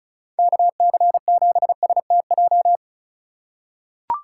Morse code, heard as a keyed tone, about 34 wpm, 690 Hz. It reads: KC7STJ
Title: USA Call Signs - 4x Character Spacing - 35wpm